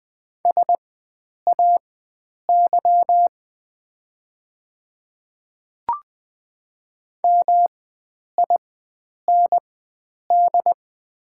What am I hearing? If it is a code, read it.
Morse code, heard as a keyed tone, about 20 wpm, spaced as for 9 wpm, 700 Hz. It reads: SAY MIND
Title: 4x Character Spacing - Top 500 Words - 20wpm